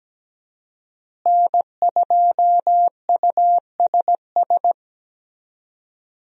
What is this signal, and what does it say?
Morse code, heard as a keyed tone, about 17 wpm, 700 Hz. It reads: N2USS
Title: USA Call Signs - 5 Characters - Repeated 2x - 17wpm